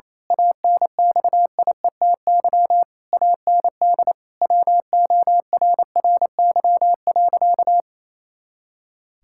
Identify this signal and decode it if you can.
Morse code, heard as a keyed tone, about 28 wpm, 700 Hz. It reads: ANXIETY AND WORRY.
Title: Pangram Challenge v2 - 28wpm